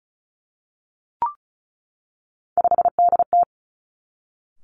Morse code, heard as a keyed tone, about 36 wpm, 700 Hz. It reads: 5DT
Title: Three Challenging Letter-Number Combinations 35wpm